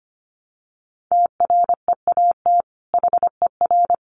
Morse code, heard as a keyed tone, about 25 wpm, 700 Hz. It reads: TREAT HER
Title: Sentences from Top 1300 Words 25wpm